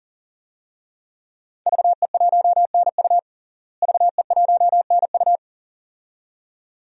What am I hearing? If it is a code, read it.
Morse code, heard as a keyed tone, about 40 wpm, 700 Hz. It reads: VE1NU VE1NU
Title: DXCC-100 Call Signs - 5 Characters - Repeated 2x - 40wpm